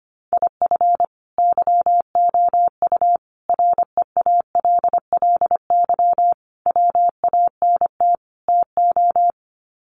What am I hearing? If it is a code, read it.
Morse code, heard as a keyed tone, about 25 wpm, 700 Hz. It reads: IF YOU REALLY WANT TO
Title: Sets of 5 Words Encore2 25wpm